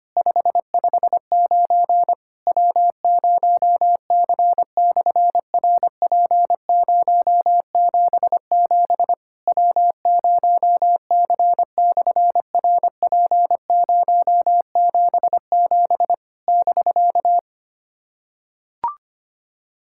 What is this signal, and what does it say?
Morse code, heard as a keyed tone, about 24 wpm, 700 Hz. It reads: 559 W0C/RP077 W0C/RP077 <BK>
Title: SOTA QSO Practice 25wpm